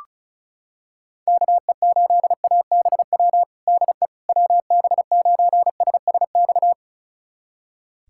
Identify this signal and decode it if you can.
Morse code, heard as a keyed tone, about 34 wpm, 700 Hz. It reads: KE8ABW DE WB9SSX
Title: USA Callsign DE Callsign - 6 Characters - 35wpm